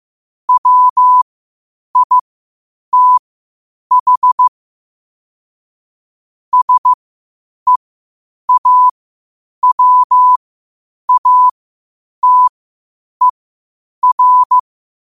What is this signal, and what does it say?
Morse code, heard as a keyed tone, about 15 wpm, spaced as for 8 wpm, 1000 Hz. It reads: WITH SEAWATER
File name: qotdfiles / 8wpm